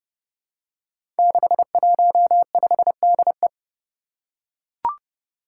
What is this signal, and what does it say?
Morse code, heard as a keyed tone, about 29 wpm, 700 Hz. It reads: 615DE
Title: Five Challenging Letter-Number Combinations 30wpm